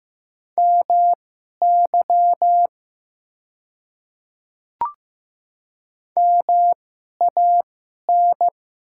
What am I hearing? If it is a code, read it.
Morse code, heard as a keyed tone, about 15 wpm, spaced as for 11 wpm, 700 Hz. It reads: MY MAN
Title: 2x Character Spacing - Top 200 Words - 15wpm